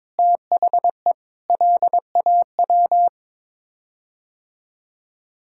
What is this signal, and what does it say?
Morse code, heard as a keyed tone, about 22 wpm, 700 Hz. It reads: THE LAW
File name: practice / 22wpm / Sets of 7 Words - 22wpm